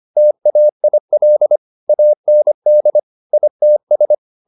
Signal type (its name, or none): Morse code